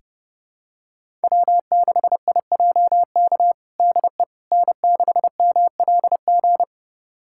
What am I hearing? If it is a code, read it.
Morse code, heard as a keyed tone, about 30 wpm, 700 Hz. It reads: W6IJK DE N6MLG
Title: USA Callsign DE Callsign - 5 Characters - 30wpm